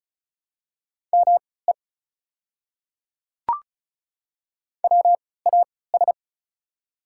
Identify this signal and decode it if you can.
Morse code, heard as a keyed tone, about 36 wpm, spaced as for 20 wpm, 700 Hz. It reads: ME WAS
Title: 3x Character Spacing - Top 200 Words - 35wpm